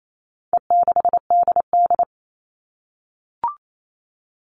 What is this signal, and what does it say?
Morse code, heard as a keyed tone, about 28 wpm, 700 Hz. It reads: E6DD
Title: All Cut Numbers - 4-Digit 28wpm